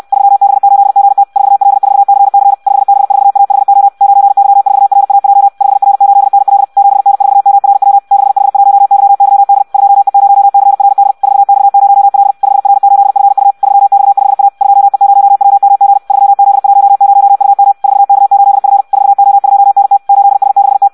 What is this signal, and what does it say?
Morse code, heard as a keyed tone, about 64 wpm, 800 Hz. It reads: 3B8GT 4L4CC 5B4AHO 9Y4RAJ 5R8NL 7N4MRW 6H1YYD 2E0CRD 4X0L 5R8FU 3X5A 2E0WMG 3Z10UM 4K8F 4O2NT 8S6T
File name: Audio practica CW / qrq-tx-66 2